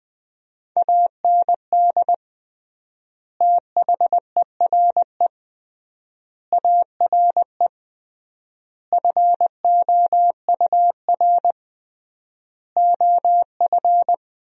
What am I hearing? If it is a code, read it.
Morse code, heard as a keyed tone, about 20 wpm, 700 Hz. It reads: AND THERE ARE FOUR OF
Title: 3x Word-Spacing - Sentences from Top 300 Words - 20wpm